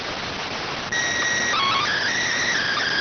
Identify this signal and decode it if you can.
SSTV (slow-scan television), PD 290.